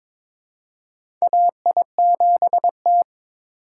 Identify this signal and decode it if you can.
Morse code, heard as a keyed tone, about 22 wpm, 700 Hz. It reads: AI7T
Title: USA Call Signs - 4x Character Spacing - 22wpm